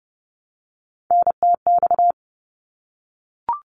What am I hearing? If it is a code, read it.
Morse code, heard as a keyed tone, about 30 wpm, 700 Hz. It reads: NTX
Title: ARRL Sections 30wpm